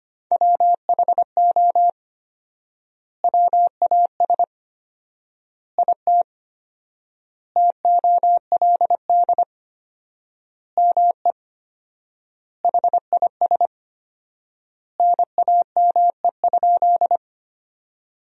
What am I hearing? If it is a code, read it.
Morse code, heard as a keyed tone, about 25 wpm, 700 Hz. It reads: WHO WAS IT TOLD ME HIS NAME?
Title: 4x Word-Spacing - Sentences from Top 400 Words - 25wpm